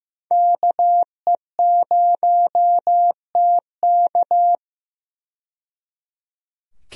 Morse code, heard as a keyed tone, about 15 wpm, 700 Hz. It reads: KE0TK
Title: USA Call Signs - 5 Characters - 15wpm